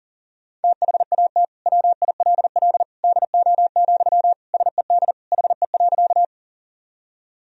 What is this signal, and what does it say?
Morse code, heard as a keyed tone, about 40 wpm, 700 Hz. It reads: THAT WILL DO, SED HE.
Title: Sentences from Top 100 Words and Abbreviations 40wpm